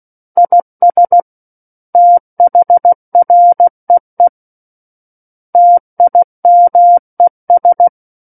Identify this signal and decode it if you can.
Morse code, heard as a keyed tone, about 16 wpm, 700 Hz. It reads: IS THREE TIMES
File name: Book One / 07-Chapter 7 - 16 WPM